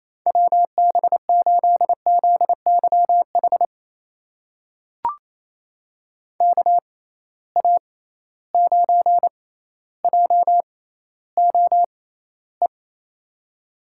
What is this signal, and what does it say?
Morse code, heard as a keyed tone, about 28 wpm, spaced as for 14 wpm, 700 Hz. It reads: WB8ZYH KA9JOE
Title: USA Call Signs - 6 Characters - 6x Character Spacing - 28wpm